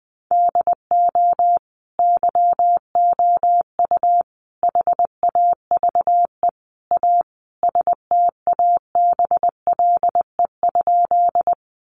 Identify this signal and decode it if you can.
Morse code, heard as a keyed tone, about 20 wpm, 700 Hz. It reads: DO YOU HAVE A STABLE?
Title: Sentences from Top 1500 Words 20wpm